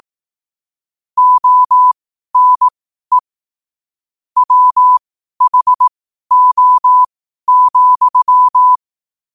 Morse code, heard as a keyed tone, about 18 wpm, spaced as for 12 wpm, 1000 Hz. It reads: ONE WHO,